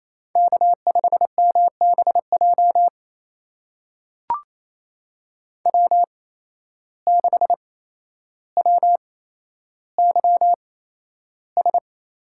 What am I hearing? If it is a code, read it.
Morse code, heard as a keyed tone, about 28 wpm, spaced as for 12 wpm, 700 Hz. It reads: K5MBJ W6WYS
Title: USA Call Signs - 5 Characters - 8x Character Spacing - 28wpm